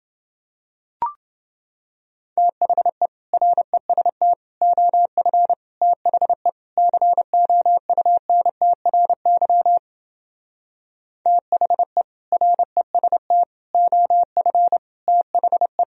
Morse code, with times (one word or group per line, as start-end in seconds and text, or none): 2.4-3.1 THE
3.3-4.3 REST
4.6-5.5 OF
5.8-6.5 THE
6.8-9.8 COUNTRY
11.3-12.0 THE
12.3-13.4 REST
13.7-14.8 OF
15.1-15.8 THE